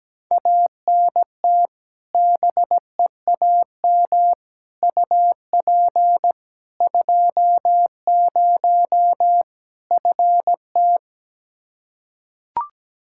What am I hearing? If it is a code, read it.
Morse code, heard as a keyed tone, about 17 wpm, 700 Hz. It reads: ANT BEAM UP 20 FT
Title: Practice QSO - Exchange 2 - Info - 17wpm